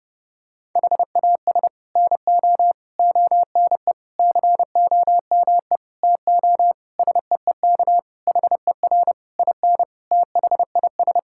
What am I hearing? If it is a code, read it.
Morse code, heard as a keyed tone, about 30 wpm, 700 Hz. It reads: HAS NO ONE COME TO SEEK HER IN THIS
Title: Sentences from Top 1600 Words 30wpm